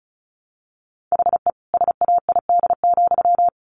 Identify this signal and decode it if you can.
Morse code, heard as a keyed tone, about 34 wpm, 700 Hz. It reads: HE SAID,
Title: Sentences from Top 600 Words 35wpm